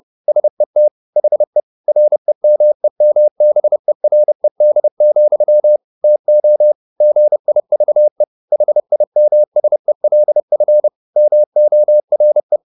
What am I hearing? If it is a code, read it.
Morse code, heard as a keyed tone, about 30 wpm, 600 Hz. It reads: SET HE REMEMBERED, TO GIVE HIMSELF MORE
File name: the old man and the sea / the old man and the sea 1 30wpm